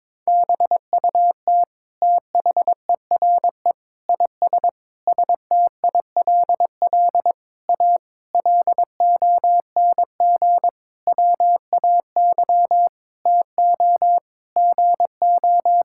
Morse code, heard as a keyed tone, about 21 wpm, 700 Hz. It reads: BUT THERE IS STILL A LONG WAY TO GO
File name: Sentences with 9-Word Phrases - 22wpm